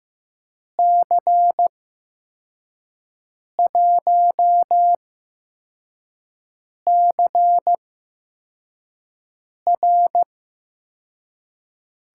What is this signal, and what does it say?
Morse code, heard as a keyed tone, about 15 wpm, spaced as for 4 wpm, 700 Hz. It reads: C1CR
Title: USA Call Signs - 6 Characters - 8x Character Spacing - 15wpm